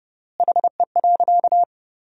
Morse code, heard as a keyed tone, about 30 wpm, 700 Hz. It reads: HE.
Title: 4x Word-Spacing - Sentences from Top 400 Words - 30wpm